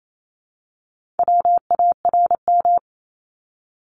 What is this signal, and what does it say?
Morse code, heard as a keyed tone, about 28 wpm, 700 Hz. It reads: WARM
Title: Top 500 Words 28wpm